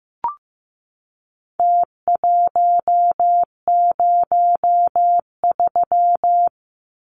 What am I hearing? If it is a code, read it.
Morse code, heard as a keyed tone, about 15 wpm, 700 Hz. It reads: T103
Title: Numbers - Contest Style - 4-Digit 15wpm